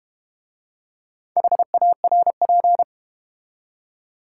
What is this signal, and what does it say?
Morse code, heard as a keyed tone, about 32 wpm, 690 Hz. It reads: HARP